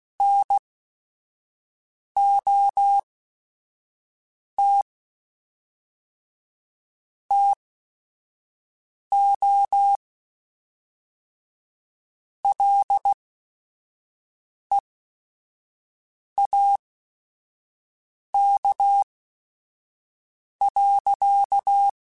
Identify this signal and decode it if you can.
Morse code, heard as a keyed tone, about 16 wpm, spaced as for 5 wpm, 800 Hz. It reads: NOT TO LEAK.